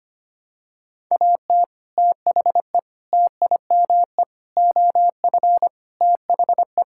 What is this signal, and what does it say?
Morse code, heard as a keyed tone, about 25 wpm, 700 Hz. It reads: AT THE TIME OF THE
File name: practice / 25wpm / Sentences with 4-Word Phrases - 25wpm